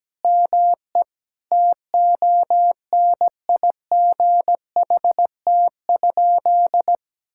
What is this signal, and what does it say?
Morse code, heard as a keyed tone, about 17 wpm, 700 Hz. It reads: ME TONIGHT?
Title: Sentences from Top 1600 Words 17wpm